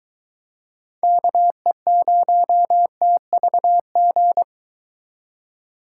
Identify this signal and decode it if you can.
Morse code, heard as a keyed tone, about 22 wpm, 700 Hz. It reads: KE0TVG